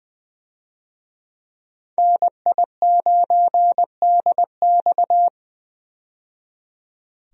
Morse code, heard as a keyed tone, about 20 wpm, 700 Hz. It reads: NI9DX
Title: USA Call Signs - 5 Characters - 20wpm